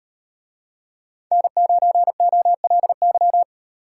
Morse code, heard as a keyed tone, about 38 wpm, 700 Hz. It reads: N9OLY